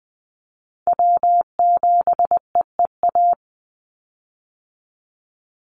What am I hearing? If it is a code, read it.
Morse code, heard as a keyed tone, about 20 wpm, 700 Hz. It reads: W7EEA
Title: USA Call Signs 20wpm